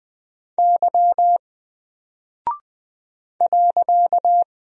Morse code, heard as a keyed tone, about 20 wpm, 700 Hz. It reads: Y .
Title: Single Character (Letter:Number:Punctuation:Prosign) 20wpm